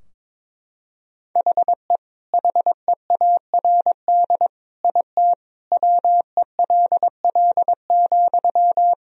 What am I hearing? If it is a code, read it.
Morse code, heard as a keyed tone, about 22 wpm, 700 Hz. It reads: HE HEARD IT WELL,
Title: Sentences from Top 1000 Words - Review Part 3 - 22wpm